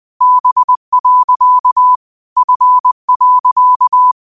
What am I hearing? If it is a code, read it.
Morse code, heard as a keyed tone, about 20 wpm, 1000 Hz. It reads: B. F.